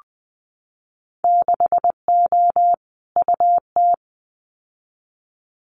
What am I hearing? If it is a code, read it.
Morse code, heard as a keyed tone, about 20 wpm, 700 Hz. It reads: 6O UT
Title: Winter Field Day Exchanges Encore 20wpm